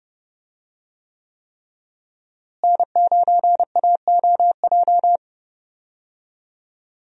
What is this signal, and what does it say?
Morse code, heard as a keyed tone, about 30 wpm, 700 Hz. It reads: N9AOJ
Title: USA Call Signs - 5 Characters - Repeated 3x - 30wpm